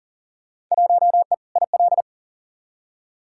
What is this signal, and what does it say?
Morse code, heard as a keyed tone, about 40 wpm, 700 Hz. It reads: 1E IL
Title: Field Day Exchanges - 40wpm